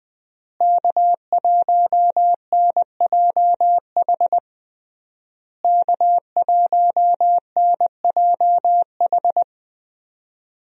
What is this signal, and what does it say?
Morse code, heard as a keyed tone, about 19 wpm, 700 Hz. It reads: K1NJH K1NJH
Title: USA Call Signs - 5 Characters - Repeated 3x - 20wpm